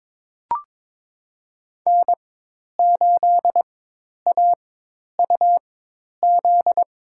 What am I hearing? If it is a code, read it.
Morse code, heard as a keyed tone, about 23 wpm, spaced as for 10 wpm, 700 Hz. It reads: N8AUZ
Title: USA Call Signs - 5 Characters - 4x Character Spacing - 22wpm